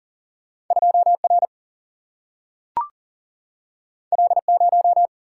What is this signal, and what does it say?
Morse code, heard as a keyed tone, about 40 wpm, 700 Hz. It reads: 2 R L 0
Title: Single-Letter Single-Number 40wpm